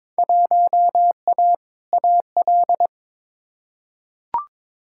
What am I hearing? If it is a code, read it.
Morse code, heard as a keyed tone, about 22 wpm, 700 Hz. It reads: 1A AL
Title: Field Day Exchanges - 22wpm